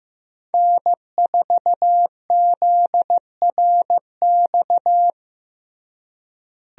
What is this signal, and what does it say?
Morse code, heard as a keyed tone, about 15 wpm, 700 Hz. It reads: N4ZRX
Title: USA Call Signs - 5 Characters - Repeated 3x - 15wpm